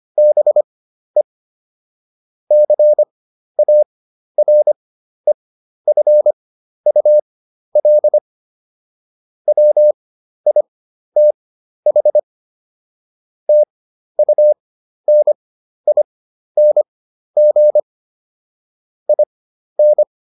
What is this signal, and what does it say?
Morse code, heard as a keyed tone, about 25 wpm, spaced as for 12 wpm, 600 Hz. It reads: BE CAREFUL WITH TUNING IN